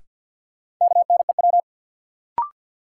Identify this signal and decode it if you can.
Morse code, heard as a keyed tone, about 53 wpm, 700 Hz. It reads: KNEW
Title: Sentences from Top 400 Words - New Word Drill - 50wpm